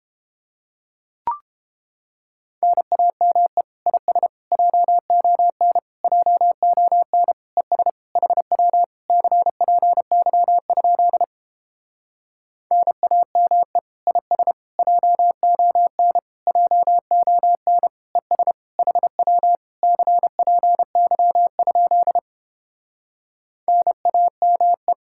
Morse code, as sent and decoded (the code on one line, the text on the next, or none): -. .- -- . / .. ... / .--- --- -. / .--- --- -. / . ... / .... .-- / -.-. .--. -.-- ..--.. / -. .- -- . / .. ... / .--- --- -. / .--- --- -. / . ... / .... .-- / -.-. .--. -.-- ..--.. / -. .- -- .
NAME IS JON JON ES HW CPY? NAME IS JON JON ES HW CPY? NAME